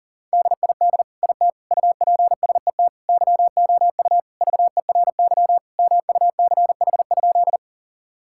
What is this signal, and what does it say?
Morse code, heard as a keyed tone, about 40 wpm, 700 Hz. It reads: DID IT UPSET YOU VERY MUCH?
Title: Sentences from Top 1000 Words - Review Part 5 - 40wpm